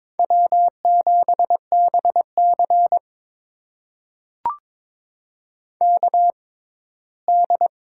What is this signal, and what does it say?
Morse code, heard as a keyed tone, about 22 wpm, spaced as for 15 wpm, 700 Hz. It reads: W7BC KD
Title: USA Call Signs - 4 Characters - 6x Character Spacing - 22wpm